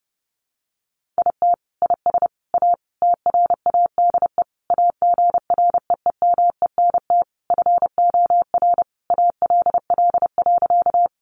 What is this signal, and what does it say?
Morse code, heard as a keyed tone, about 30 wpm, 700 Hz. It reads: IT IS A TRADE AGREEMENT FOR ALL.
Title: Sentences from Top 1500 Words 30wpm